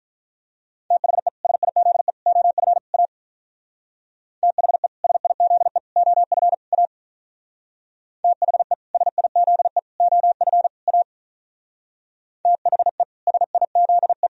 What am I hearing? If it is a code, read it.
Morse code, heard as a keyed tone, about 45 wpm, 700 Hz. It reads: THE SIZE OF A THE SIZE OF A THE SIZE OF A THE SIZE